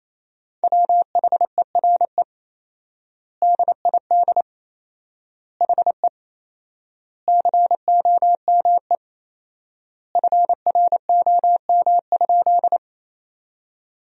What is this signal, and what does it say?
Morse code, heard as a keyed tone, about 28 wpm, 700 Hz. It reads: WHERE DID HE COME FROM?
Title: 4x Word-Spacing - Sentences from Top 200 Words - 28wpm